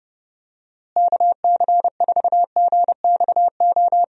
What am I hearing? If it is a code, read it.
Morse code, heard as a keyed tone, about 32 wpm, 700 Hz. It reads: KC4GXO
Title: USA Call Signs - 6 Characters - Repeated 3x - 30wpm